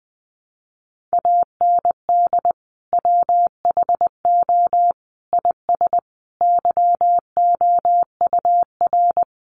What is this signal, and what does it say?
Morse code, heard as a keyed tone, about 20 wpm, 700 Hz. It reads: AND WHO IS YOUR
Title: Sentences from Top 1300 Words 20wpm